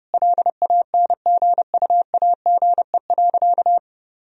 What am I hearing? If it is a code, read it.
Morse code, heard as a keyed tone, about 30 wpm, 700 Hz. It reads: LANGUAGE.
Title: Sentences from Top 1000 Words - Review Part 3 - 30wpm